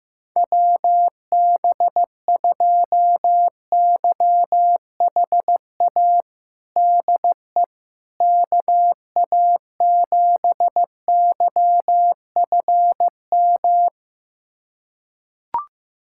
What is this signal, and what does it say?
Morse code, heard as a keyed tone, about 15 wpm, 700 Hz. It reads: WB2YHA DE KA7YFM
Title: USA Callsign DE Callsign - 6 Characters - 15wpm